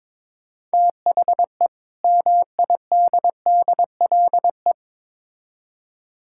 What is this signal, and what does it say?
Morse code, heard as a keyed tone, about 22 wpm, 700 Hz. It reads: THE MIDDLE
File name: Build A Phrase - 2-Word Phrases - 22wpm